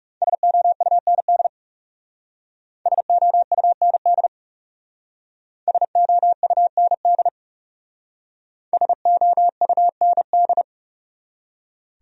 Morse code, heard as a keyed tone, about 41 wpm, 700 Hz. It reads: SOUND SOUND SOUND SOUND